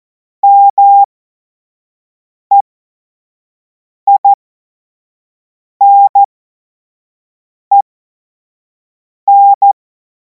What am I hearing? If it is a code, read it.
Morse code, heard as a keyed tone, about 14 wpm, spaced as for 5 wpm, 800 Hz. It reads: MEINEN
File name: Morsekurs / Witze / Witz 23. Juni 2026 25 F70